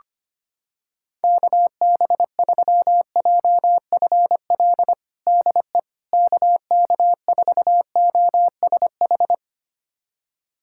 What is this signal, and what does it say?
Morse code, heard as a keyed tone, about 25 wpm, 700 Hz. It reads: KB3JFL DE KK4OSH
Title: USA Callsign DE Callsign 25wpm